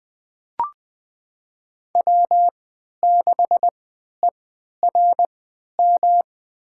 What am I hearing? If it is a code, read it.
Morse code, heard as a keyed tone, about 20 wpm, spaced as for 11 wpm, 700 Hz. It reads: W6ERM
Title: USA Call Signs - 5 Characters - 3x Character Spacing - 20wpm